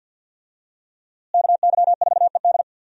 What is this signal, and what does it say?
Morse code, heard as a keyed tone, about 50 wpm, 700 Hz. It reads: KY4ED
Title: USA Call Signs - 5 Characters - Repeated 2x - 50wpm